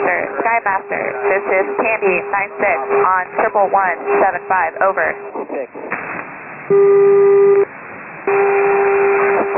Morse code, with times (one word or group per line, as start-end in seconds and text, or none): none